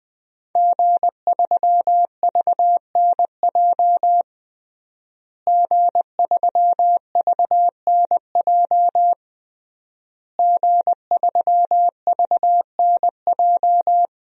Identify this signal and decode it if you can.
Morse code, heard as a keyed tone, about 21 wpm, 700 Hz. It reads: G3VNJ G3VNJ G3VNJ
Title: DXCC-100 Call Signs - 5 Characters - Repeated 3x - 20wpm